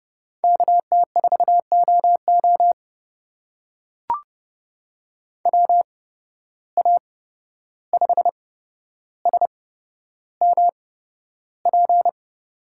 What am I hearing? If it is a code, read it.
Morse code, heard as a keyed tone, about 30 wpm, spaced as for 12 wpm, 700 Hz. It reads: KT4OO WA5SMP
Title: USA Call Signs - 8x Character Spacing - 30wpm